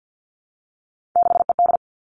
Morse code, heard as a keyed tone, about 50 wpm, 700 Hz. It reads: 6ED
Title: Three Challenging Letter-Number Combinations 50wpm